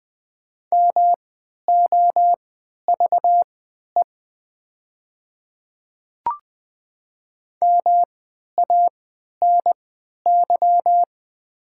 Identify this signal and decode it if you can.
Morse code, heard as a keyed tone, about 20 wpm, spaced as for 11 wpm, 700 Hz. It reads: MOVE MANY
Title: 3x Character Spacing - Top 200 Words - 20wpm